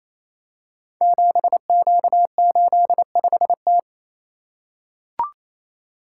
Morse code, heard as a keyed tone, about 27 wpm, 700 Hz. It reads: 7Q85T
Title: Code Groups - 5 Letter:Numbers - 28wpm